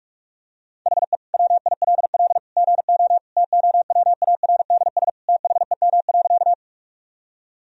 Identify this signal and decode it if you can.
Morse code, heard as a keyed tone, about 46 wpm, 700 Hz. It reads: HE WILL GO TOWARDS THEM.
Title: Sentences from Top 1000 Words - Review Part 2 - 45wpm